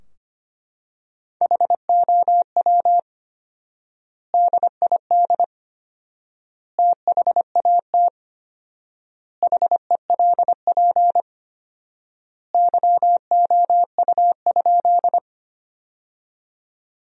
Morse code, heard as a keyed tone, about 25 wpm, 700 Hz. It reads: HOW DID THAT HELP YOU?
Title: 4x Word-Spacing - Sentences from Top 200 Words - 25wpm